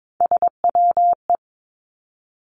Morse code, heard as a keyed tone, about 22 wpm, 700 Hz. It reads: SWE
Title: Three Challenging Letter-Number Combinations 22wpm